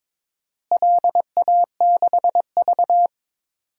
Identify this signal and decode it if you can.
Morse code, heard as a keyed tone, about 22 wpm, 700 Hz. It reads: LA6V
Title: DXCC-100 Call Signs - 4 Characters - Repeated 2x - 22wpm